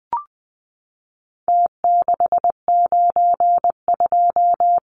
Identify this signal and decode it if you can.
Morse code, heard as a keyed tone, about 20 wpm, 700 Hz. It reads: T692
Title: Numbers - Contest Style - 4-Digit 20wpm